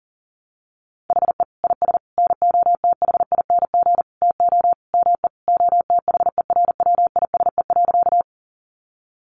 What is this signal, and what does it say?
Morse code, heard as a keyed tone, about 40 wpm, 700 Hz. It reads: HE IS NOTHING TO ME OTHERWISE.
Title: Sentences from Top 1200 Words 40wpm